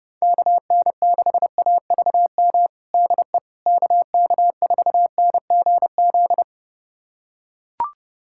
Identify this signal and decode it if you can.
Morse code, heard as a keyed tone, about 29 wpm, 700 Hz. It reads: KN6AVM DE KK4NGZ
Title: USA Callsign DE Callsign 30wpm